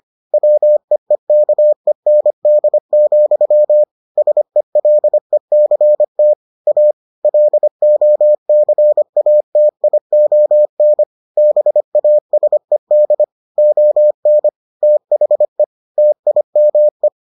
Morse code, heard as a keyed tone, about 25 wpm, 600 Hz. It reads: WEEKEND, SELECT A LOCATION BASED ON THE TIME